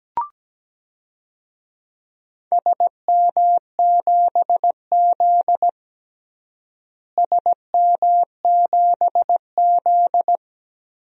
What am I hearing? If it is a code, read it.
Morse code, heard as a keyed tone, about 17 wpm, 700 Hz. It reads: SM7Z SM7Z
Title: DXCC-100 Call Signs - 4 Characters - Repeated 2x - 17wpm